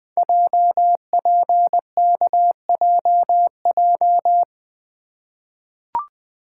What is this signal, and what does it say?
Morse code, heard as a keyed tone, about 20 wpm, 700 Hz. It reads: JPKJJ
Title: Code Groups - 5 Letter:Numbers - 20wpm